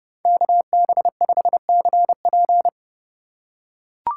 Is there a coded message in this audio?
Yes: Morse code from 0.2 to 2.7 s, heard as a keyed tone, about 30 wpm, 700 Hz. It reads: KB5CP